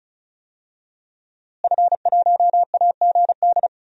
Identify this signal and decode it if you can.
Morse code, heard as a keyed tone, about 36 wpm, 700 Hz. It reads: F1AGD
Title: DXCC-100 Call Signs - 5 Characters - Repeated 3x - 35wpm